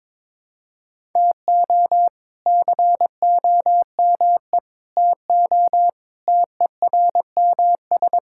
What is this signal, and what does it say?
Morse code, heard as a keyed tone, about 22 wpm, 700 Hz. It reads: TO COME TO TERMS
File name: practice / 22wpm / Sets of 6 Words - 22wpm